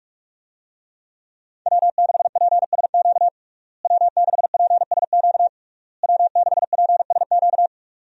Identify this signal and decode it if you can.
Morse code, heard as a keyed tone, about 46 wpm, 700 Hz. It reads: W6PSQ W6PSQ W6PSQ
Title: USA Call Signs - 5 Characters - Repeated 3x - 45wpm